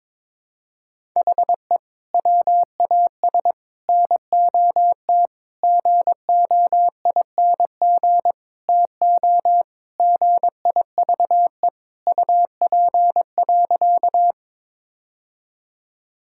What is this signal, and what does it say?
Morse code, heard as a keyed tone, about 22 wpm, 700 Hz. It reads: HE WAS NOT GOING TO GIVE UP.
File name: practice / 22wpm / Sentences with 4-Word Phrases - 22wpm